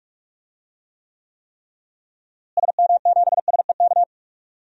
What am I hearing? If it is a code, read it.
Morse code, heard as a keyed tone, about 46 wpm, 700 Hz. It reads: SM7SEK
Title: DXCC-100 Call Signs - 6 Characters - Repeated 3x - 45wpm